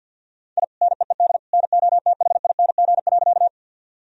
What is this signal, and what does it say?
Morse code, heard as a keyed tone, about 52 wpm, 700 Hz. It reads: I NEED NOTHING.
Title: Sentences from Top 1000 Words - Review Part 1 - 50wpm